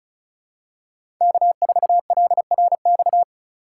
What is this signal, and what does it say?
Morse code, heard as a keyed tone, about 36 wpm, 700 Hz. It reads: K4LRX
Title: K1USN SST QSO Practice - 35wpm